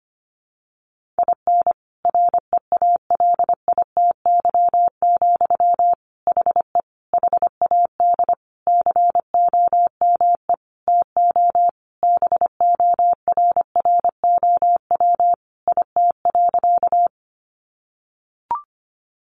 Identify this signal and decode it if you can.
Morse code, heard as a keyed tone, about 25 wpm, 700 Hz. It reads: IN REALITY, HE HAD COME TO BORROW IT.
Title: Sentences from Top 1500 Words 25wpm